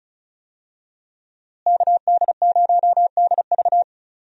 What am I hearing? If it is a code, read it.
Morse code, heard as a keyed tone, about 36 wpm, 700 Hz. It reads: KD0DV
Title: USA Call Signs - 5 Characters - Repeated 3x - 35wpm